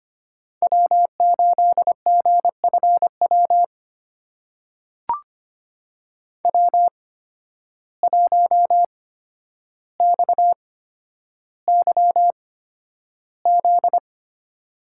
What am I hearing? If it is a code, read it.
Morse code, heard as a keyed tone, about 25 wpm, spaced as for 11 wpm, 700 Hz. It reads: W8GFW W1XYZ
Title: USA Call Signs - 5 Characters - 8x Character Spacing - 25wpm